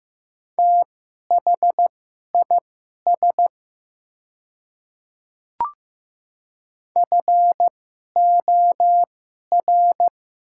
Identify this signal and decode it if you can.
Morse code, heard as a keyed tone, about 15 wpm, spaced as for 11 wpm, 700 Hz. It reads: THIS FOR
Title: 2x Character Spacing - Top 100 Words - 15wpm